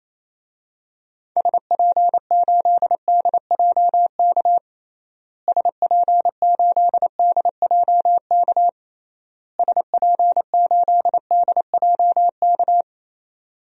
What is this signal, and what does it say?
Morse code, heard as a keyed tone, about 28 wpm, 700 Hz. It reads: SP8DJK SP8DJK SP8DJK
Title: DXCC-100 Call Signs - 6 Characters - Repeated 3x - 28wpm